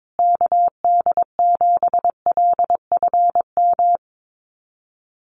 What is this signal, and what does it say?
Morse code, heard as a keyed tone, about 22 wpm, 700 Hz. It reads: KD7LFM
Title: USA Call Signs Repeated 2x 22wpm